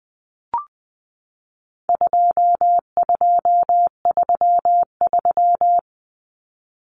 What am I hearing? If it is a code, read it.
Morse code, heard as a keyed tone, about 20 wpm, 700 Hz. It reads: 2233
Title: Common Cut Numbers - 4-Digit 20wpm